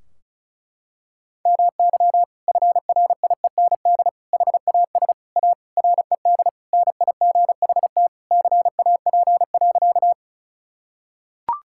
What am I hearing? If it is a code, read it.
Morse code, heard as a keyed tone, about 36 wpm, 700 Hz. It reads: MY FRIEND HAS A RED NIGHT CAP.
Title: Sentences from Top 1600 Words 35wpm